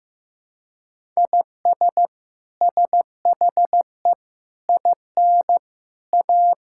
Morse code, heard as a keyed tone, about 15 wpm, 700 Hz. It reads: IS SHE IN A
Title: Sentences from Top 1000 Words - Review Part 6 - 15wpm